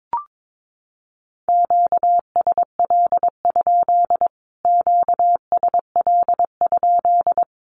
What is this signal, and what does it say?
Morse code, heard as a keyed tone, about 22 wpm, 700 Hz. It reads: QSL? QSL?
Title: Q-Codes Repeated 3x 22wpm